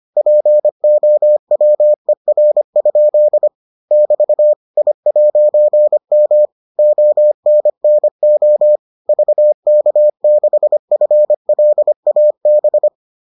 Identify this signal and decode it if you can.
Morse code, heard as a keyed tone, about 25 wpm, 600 Hz. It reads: POWER? <BT> I'M ONNO VK6FLAB